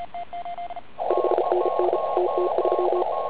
Morse code, heard as a keyed tone, about 34 wpm, 360 Hz, with radio noise. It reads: 5NN TT3
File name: Archiv / CUC140 / OK1IF CUC140